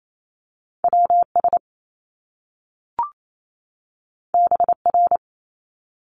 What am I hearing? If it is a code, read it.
Morse code, heard as a keyed tone, about 28 wpm, 700 Hz. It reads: WS BR
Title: Top Two Letter Combinations in Words 28wpm